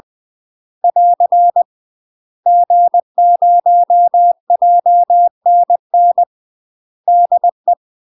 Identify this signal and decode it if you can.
Morse code, heard as a keyed tone, about 20 wpm, 700 Hz. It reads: <AR> G0JNN DE
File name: UK MASTER QSOs TRAINING / 20WPM / UK MASTER QSOs 0012